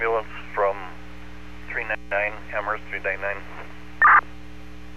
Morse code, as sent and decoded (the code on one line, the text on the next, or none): none